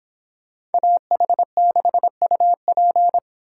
Morse code, heard as a keyed tone, about 29 wpm, 700 Hz. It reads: AH6UP